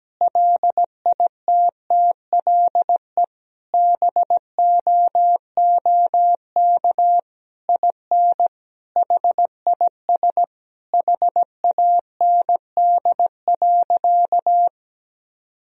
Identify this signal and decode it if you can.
Morse code, heard as a keyed tone, about 17 wpm, 700 Hz. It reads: LITTLE BOOK IN HIS HAND.
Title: Sentences from Top 1000 Words - Review Part 1 - 17wpm